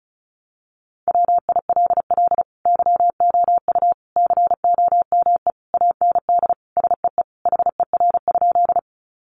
Morse code, heard as a keyed tone, about 34 wpm, 700 Hz. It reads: WILL YOU COME AND SEE HER?
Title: Sentences from Top 100 Words 35wpm